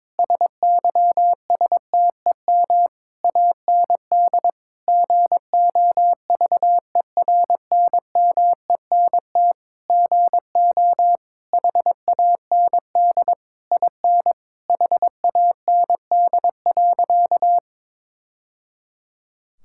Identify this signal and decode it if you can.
Morse code, heard as a keyed tone, about 22 wpm, 700 Hz. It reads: SYSTEM AND GOVERNMENT GO HAND IN HAND.
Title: Sentences from Top 1000 Words - Review Part 5 - 22wpm